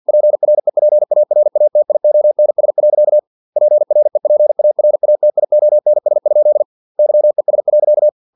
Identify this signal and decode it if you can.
Morse code, heard as a keyed tone, about 50 wpm, 600 Hz. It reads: PREPARATIONS. PREPARATIONS? YES.
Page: Never 49wpm